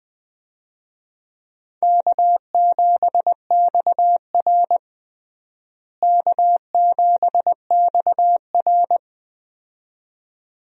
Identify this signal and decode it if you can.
Morse code, heard as a keyed tone, about 20 wpm, 700 Hz. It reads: K7XR K7XR
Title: USA Call Signs - 4 Characters - Repeated 2x - 20wpm